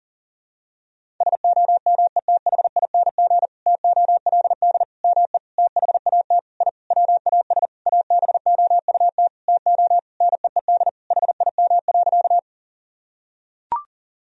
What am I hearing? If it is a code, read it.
Morse code, heard as a keyed tone, about 40 wpm, 700 Hz. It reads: SOMETHING TOLD ME THAT I WAS ABOUT TO NEED HIM.
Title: Sentences from Top 1000 Words - Review Part 1 - 40wpm